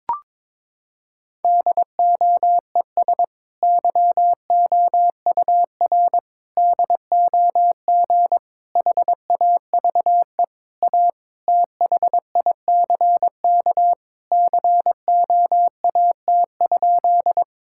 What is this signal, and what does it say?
Morse code, heard as a keyed tone, about 22 wpm, 700 Hz. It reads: DOES YOUR DOG HAVE A THICK COAT?
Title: Sentences from Top 1600 Words 22wpm